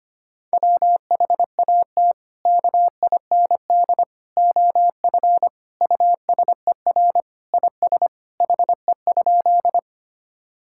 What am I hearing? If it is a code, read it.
Morse code, heard as a keyed tone, about 25 wpm, 700 Hz. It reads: WHAT KIND OF USER IS HE?
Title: Sentences from Top 1000 Words - Review Part 5 - 25wpm